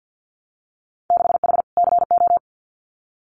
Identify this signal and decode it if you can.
Morse code, heard as a keyed tone, about 50 wpm, 700 Hz. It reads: 6H CO
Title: Winter Field Day Exchanges Encore 50wpm